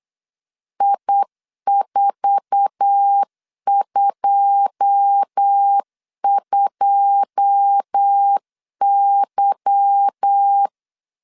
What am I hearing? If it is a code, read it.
Morse code, heard as a keyed tone, about 8 wpm, 800 Hz. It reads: I422Y
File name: LEZIONI / 9 2 8wpm 3437